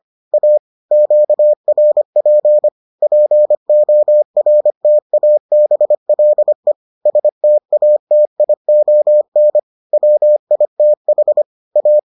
Morse code, heard as a keyed tone, about 25 wpm, 600 Hz. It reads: A QRP PORTABLE STATION WITH A